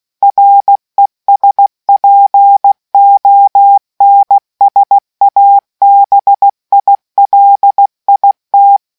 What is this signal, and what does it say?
Morse code, heard as a keyed tone, about 16 wpm, 800 Hz. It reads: RESPONSABILIT